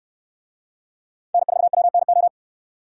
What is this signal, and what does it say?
Morse code, heard as a keyed tone, about 68 wpm, 700 Hz. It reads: N4WNO